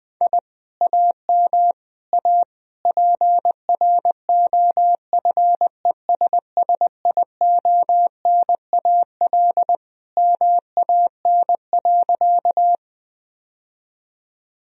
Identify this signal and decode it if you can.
Morse code, heard as a keyed tone, about 20 wpm, 700 Hz. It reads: I AM A PROFESSIONAL MAN.
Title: Sentences from Top 1000 Words - Review Part 4 - 20wpm